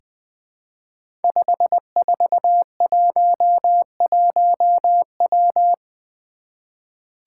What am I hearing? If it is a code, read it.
Morse code, heard as a keyed tone, about 20 wpm, 700 Hz. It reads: 5411W
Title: Five Challenging Letter-Number Combinations 20wpm